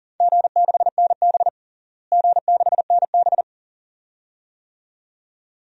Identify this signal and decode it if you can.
Morse code, heard as a keyed tone, about 40 wpm, 700 Hz. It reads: G6NB G6NB
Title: DXCC-100 Call Signs - 4 Characters - Repeated 2x - 40wpm